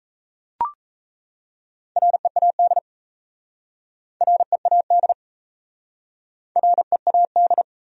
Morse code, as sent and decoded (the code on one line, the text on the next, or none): .-. .- -.. / .-. .- -.. / .-. .- -..
READ READ READ